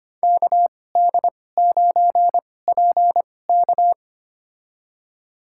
Morse code, heard as a keyed tone, about 25 wpm, spaced as for 18 wpm, 700 Hz. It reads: KD9PK
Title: USA Call Signs - 5 Characters - 2x Character Spacing - 25wpm